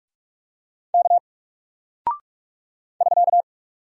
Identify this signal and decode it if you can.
Morse code, heard as a keyed tone, about 44 wpm, 700 Hz. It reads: K <SK>
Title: Single Character (Letter:Number:Punctuation:Prosign) 45wpm